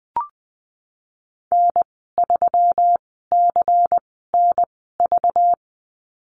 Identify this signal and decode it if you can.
Morse code, heard as a keyed tone, about 19 wpm, spaced as for 14 wpm, 700 Hz. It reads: N3CNV
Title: 200 Lesson 32g - Letter k - Call Signs 20wpm